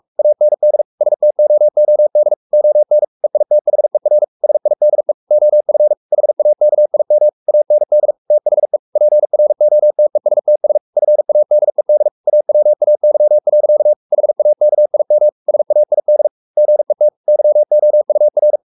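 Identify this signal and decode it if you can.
Morse code, heard as a keyed tone, about 44 wpm, 600 Hz. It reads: AND STOOD ON EITHER SIDE OF HAKIM AND THE PROTESTS FADED AWAY. HAKIM SAID GET YOUR
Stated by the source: Morse code at 43 wpm sending Ken Follett's novel Never in English, from CW med Gnister.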